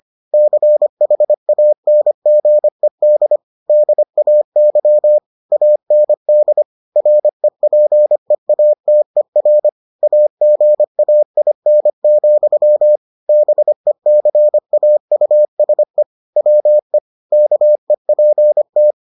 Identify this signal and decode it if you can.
Morse code, heard as a keyed tone, about 25 wpm, 600 Hz. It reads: CHANGED DAY AND REPEATER AGAIN, BECAUSE WE KEPT